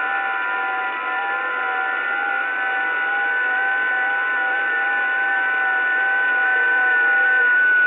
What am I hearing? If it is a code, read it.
FT8.
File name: test / websdr test15